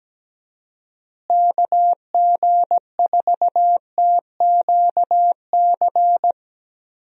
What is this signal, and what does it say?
Morse code, heard as a keyed tone, about 17 wpm, 700 Hz. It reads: KG4TQC
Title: USA Call Signs - 6 Characters - Repeated 2x - 17wpm